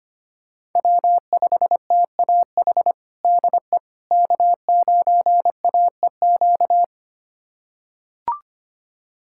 Morse code, heard as a keyed tone, about 25 wpm, 700 Hz. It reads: W5TAH DE K9AEQ
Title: USA Callsign DE Callsign - 5 Characters - 25wpm